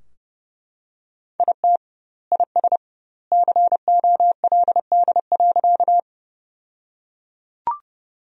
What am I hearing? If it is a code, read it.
Morse code, heard as a keyed tone, about 32 wpm, 700 Hz. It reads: IT IS COLD.
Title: 2x Word-Spacing - Sentences from Top 400 Words - 30wpm